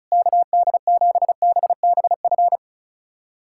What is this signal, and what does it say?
Morse code, heard as a keyed tone, about 34 wpm, 700 Hz. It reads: KD7BBF
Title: USA Callsign DE Callsign - 6 Characters - 35wpm